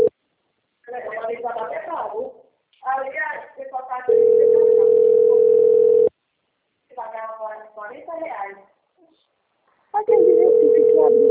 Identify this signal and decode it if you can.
Ringback tone (North America), 440 + 480 Hz, with speech.
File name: out- - - - - .1388